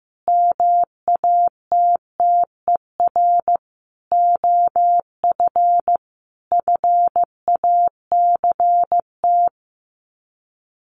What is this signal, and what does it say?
Morse code, heard as a keyed tone, about 15 wpm, 700 Hz. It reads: MATTER OF FACT
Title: Sets of 5 Words 15wpm